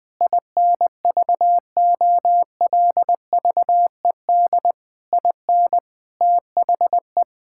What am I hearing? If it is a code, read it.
Morse code, heard as a keyed tone, about 20 wpm, 700 Hz. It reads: INVOLVED IN THE
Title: Sentences with 6-Word Phrases - 20wpm